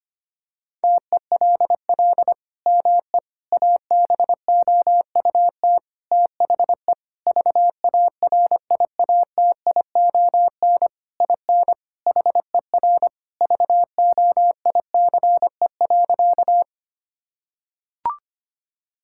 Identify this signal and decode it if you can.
Morse code, heard as a keyed tone, about 25 wpm, 700 Hz. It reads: TELL ME ABOUT THE VARIATION IN HER VOICE.
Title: Sentences from Top 1600 Words 25wpm